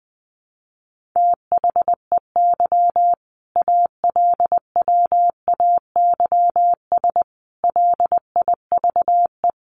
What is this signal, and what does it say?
Morse code, heard as a keyed tone, about 20 wpm, 700 Hz. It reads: THEY ALWAYS LIVE
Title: Sentences from Top 1400 Words 20wpm